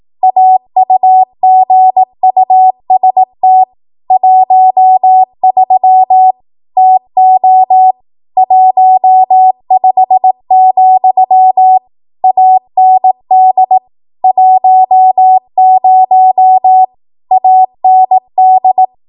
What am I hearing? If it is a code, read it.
Morse code, heard as a keyed tone, about 18 wpm, 750 Hz. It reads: AUGUST 13 TO 15, AND 10 AND
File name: Archive / 18 WPM / 18WPM